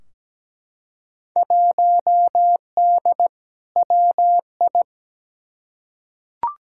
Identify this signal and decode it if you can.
Morse code, heard as a keyed tone, about 17 wpm, 700 Hz. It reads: 1D WI
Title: Field Day Exchanges - 17wpm